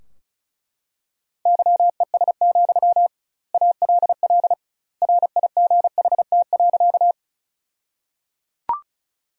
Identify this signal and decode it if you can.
Morse code, heard as a keyed tone, about 34 wpm, 700 Hz. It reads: YES, ALL RIGHT.
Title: 2x Word-Spacing - Sentences from Top 500 Words - 35wpm